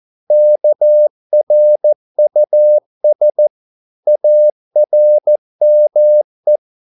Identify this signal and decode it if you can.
Morse code, heard as a keyed tone, about 14 wpm, 600 Hz. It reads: KRUS ARME
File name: Korte ord DK 14wpm